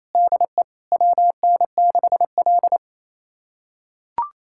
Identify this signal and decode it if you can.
Morse code, heard as a keyed tone, about 28 wpm, 700 Hz. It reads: DE WN6L
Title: USA Callsign DE Callsign - 4 Characters - 28wpm